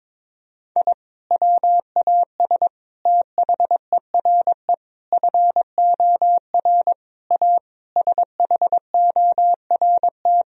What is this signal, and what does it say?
Morse code, heard as a keyed tone, about 22 wpm, 700 Hz. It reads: I WAS THERE FOR A SHORT
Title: Sentences from Top 1000 Words - Review Part 2 - 22wpm